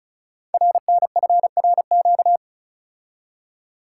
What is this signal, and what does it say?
Morse code, heard as a keyed tone, about 36 wpm, 700 Hz. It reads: RNFRQ
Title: Code Groups - 5 Letters - 35wpm